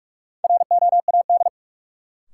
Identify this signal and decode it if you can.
Morse code, heard as a keyed tone, about 46 wpm, 700 Hz. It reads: ROAD
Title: Sentences from Top 400 Words - New Word Drill - 45wpm